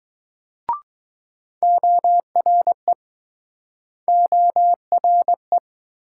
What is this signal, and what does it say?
Morse code, heard as a keyed tone, about 22 wpm, 700 Hz. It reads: ORE ORE